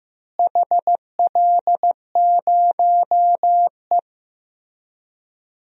Morse code, heard as a keyed tone, about 16 wpm, 700 Hz. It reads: HL0E
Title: DXCC-100 Call Signs - 4 Characters - Repeated 2x - 15wpm